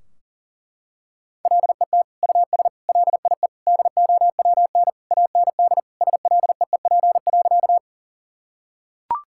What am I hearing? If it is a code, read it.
Morse code, heard as a keyed tone, about 37 wpm, 700 Hz. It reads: LET US LIE DOWN AND SLEEP.
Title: Sentences from Top 1600 Words 40wpm